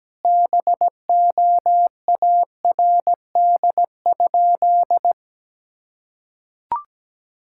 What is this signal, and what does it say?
Morse code, heard as a keyed tone, about 17 wpm, 700 Hz. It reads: BOARD?
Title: Sentences from Top 1000 Words - Review Part 6 - 17wpm